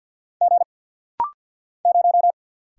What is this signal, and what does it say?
Morse code, heard as a keyed tone, about 47 wpm, 700 Hz. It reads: G 0
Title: Single Letter-Number Alt - Rapid-Fire 50wpm